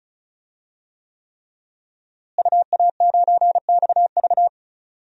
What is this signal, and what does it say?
Morse code, heard as a keyed tone, about 36 wpm, 700 Hz. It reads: UA9XV
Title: DXCC-100 Call Signs - 5 Characters - Repeated 3x - 35wpm